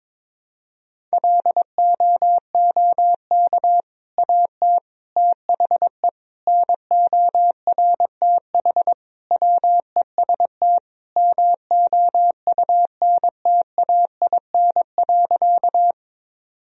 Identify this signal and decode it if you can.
Morse code, heard as a keyed tone, about 22 wpm, 700 Hz. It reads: LOOK AT THE NORTH WEST MOUNTAIN.
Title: Sentences from Top 1000 Words - Review Part 2 - 22wpm